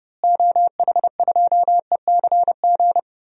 Morse code, heard as a keyed tone, about 30 wpm, 700 Hz. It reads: OH2ECG
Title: DXCC-100 Call Signs - 6 Characters - Repeated 3x - 30wpm